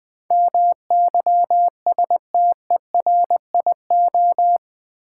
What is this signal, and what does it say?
Morse code, heard as a keyed tone, about 20 wpm, 700 Hz. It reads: MYSTERIO
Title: Build A Word - 4 Letters At A Time - 10-Letter Words - 20wpm